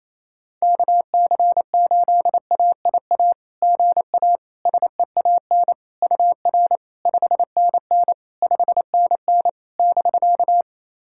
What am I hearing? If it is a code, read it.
Morse code, heard as a keyed tone, about 28 wpm, 700 Hz. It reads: KC8AIA GA SEAN UR 5NN 5NN <BK>
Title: SOTA QSO Practice 28wpm